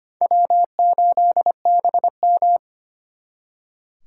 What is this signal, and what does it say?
Morse code, heard as a keyed tone, about 32 wpm, 700 Hz. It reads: W8BM